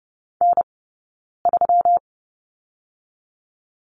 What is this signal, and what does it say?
Morse code, heard as a keyed tone, about 30 wpm, spaced as for 9 wpm, 700 Hz. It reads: N3
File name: Lesson 20d - Number 3 - Two Characters 30wpm